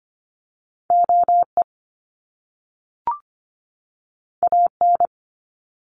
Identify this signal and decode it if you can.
Morse code, heard as a keyed tone, about 25 wpm, 700 Hz. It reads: OE AN
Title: Top Two Letter Combinations in Words 25wpm